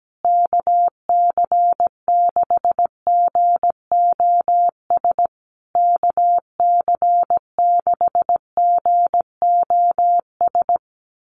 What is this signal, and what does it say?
Morse code, heard as a keyed tone, about 17 wpm, 700 Hz. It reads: KC6GOS KC6GOS
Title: USA Call Signs Repeated 3x 17wpm